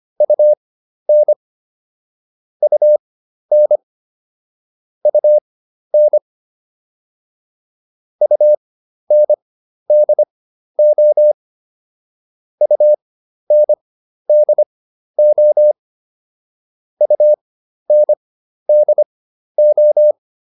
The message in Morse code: UN UN UN UNDO UNDO UNDO

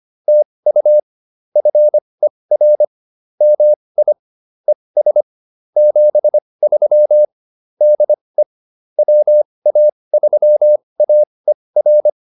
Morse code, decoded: TU FER MI ES 73 DE WA3AER

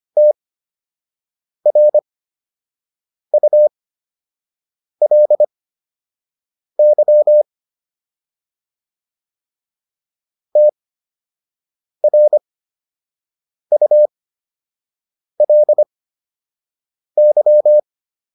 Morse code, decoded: TRULY TRULY

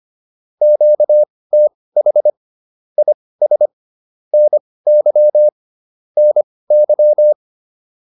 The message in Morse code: QTH IS NY NY

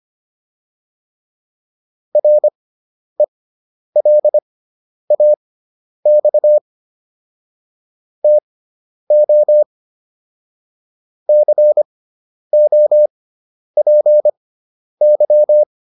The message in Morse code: RELAX TO COPY

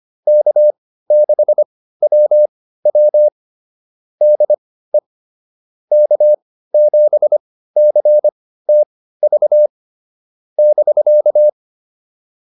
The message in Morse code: K6WW DE K7CTV <BK>

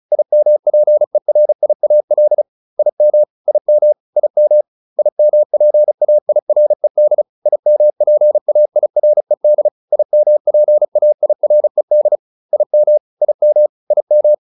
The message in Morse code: IMPERIAL IM IM IM IMPAIRED IMPAIRED IMPAIRED IM IM IM

